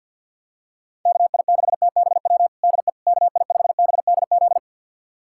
Morse code, heard as a keyed tone, about 51 wpm, 700 Hz. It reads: KI6TBW DE KI5BDZ
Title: USA Callsign DE Callsign - 6 Characters - 50wpm